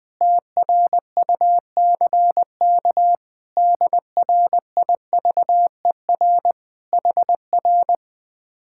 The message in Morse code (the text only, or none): TRUCK DRIVER HR